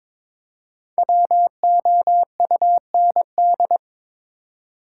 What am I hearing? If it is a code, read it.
Morse code, heard as a keyed tone, about 22 wpm, 700 Hz. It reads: WOUND
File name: 5 Letter Words - First 3 Letters Given - 22wpm